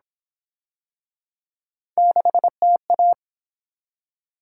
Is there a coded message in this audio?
Yes: Morse code from 2.0 to 3.1 s, heard as a keyed tone, about 29 wpm, 700 Hz. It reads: 6TA